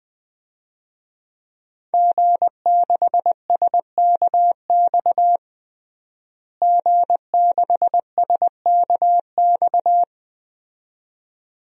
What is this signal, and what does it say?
Morse code, heard as a keyed tone, about 21 wpm, 700 Hz. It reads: G6SKX G6SKX
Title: DXCC-100 Call Signs - 5 Characters - Repeated 2x - 20wpm